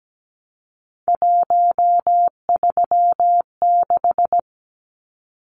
Morse code, heard as a keyed tone, about 17 wpm, 700 Hz. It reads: 136
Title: Numbers - 3-Digit 17wpm